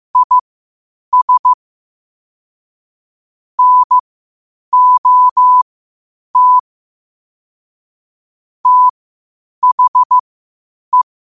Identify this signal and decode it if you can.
Morse code, heard as a keyed tone, about 15 wpm, spaced as for 8 wpm, 1000 Hz. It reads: IS NOT THE
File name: qotdfiles / 8wpm